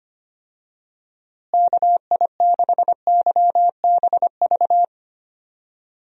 Morse code, heard as a keyed tone, about 25 wpm, 700 Hz. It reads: KI6YBV
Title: USA Call Signs - 6 Characters - Repeated 3x - 25wpm